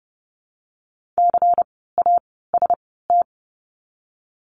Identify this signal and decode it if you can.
Morse code, heard as a keyed tone, about 30 wpm, 700 Hz. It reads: C A S T
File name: Lesson 16f - Letter c - Words 30wpm